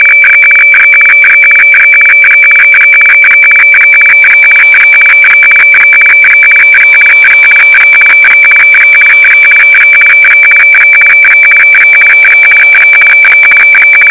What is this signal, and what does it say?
Weather fax (radiofax).